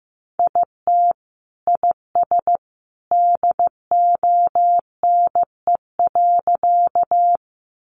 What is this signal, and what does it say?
Morse code, heard as a keyed tone, about 15 wpm, 700 Hz. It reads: IT IS DONE.
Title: Sentences from Top 500 Words 15wpm